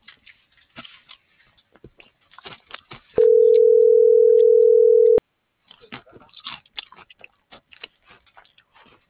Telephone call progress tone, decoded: ringback tone

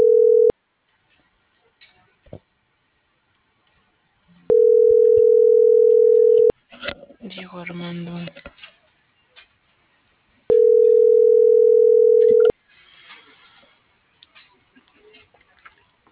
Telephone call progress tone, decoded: ringback tone

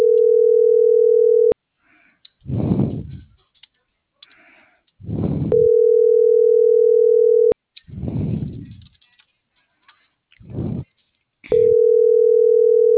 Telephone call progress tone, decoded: ringback tone

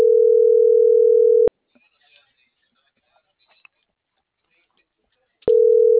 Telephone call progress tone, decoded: ringback tone

